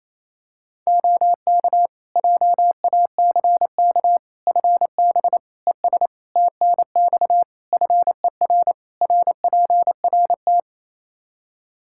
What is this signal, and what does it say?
Morse code, heard as a keyed tone, about 28 wpm, 700 Hz. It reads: OK JACK FB ES TNX FER RPRT
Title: Practice QSO - Exchange 2 - Info - 28wpm